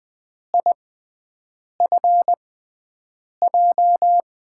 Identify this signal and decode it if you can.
Morse code, heard as a keyed tone, about 20 wpm, spaced as for 7 wpm, 700 Hz. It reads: IFJ
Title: USA Call Signs - 6 Characters - 6x Character Spacing - 20wpm